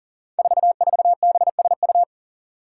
Morse code, heard as a keyed tone, about 37 wpm, 700 Hz. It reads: 44BSU
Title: Five Challenging Letter-Number Combinations 40wpm